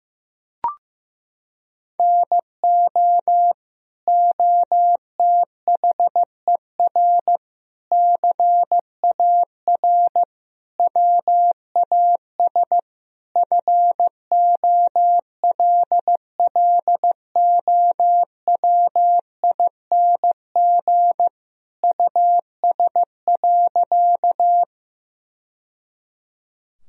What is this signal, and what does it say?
Morse code, heard as a keyed tone, about 15 wpm, 700 Hz. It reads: NO OTHER CAR WAS FOLLOWING US.
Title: Sentences from Top 1000 Words - Review Part 4 - 15wpm